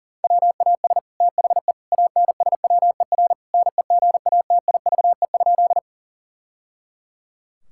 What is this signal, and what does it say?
Morse code, heard as a keyed tone, about 40 wpm, 690 Hz. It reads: WAS THE ANSWER NEGATIVE?
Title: Sentences from Top 1000 Words - Review Part 6 - 40wpm